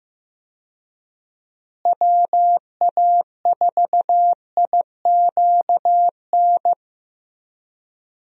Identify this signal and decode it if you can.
Morse code, heard as a keyed tone, about 15 wpm, 700 Hz. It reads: WA4IQN
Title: USA Call Signs - 6 Characters - Repeated 3x - 15wpm